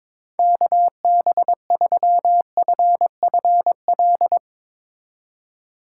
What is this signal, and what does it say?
Morse code, heard as a keyed tone, about 22 wpm, 700 Hz. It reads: KB3FFL
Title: Speed Racing - USA 2x3 Call Signs 22wpm